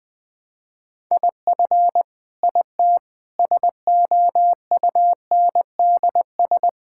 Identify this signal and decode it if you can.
Morse code, heard as a keyed tone, about 20 wpm, 700 Hz. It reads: IF IT SOUNDS
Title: Sentences with 9-Word Phrases - 20wpm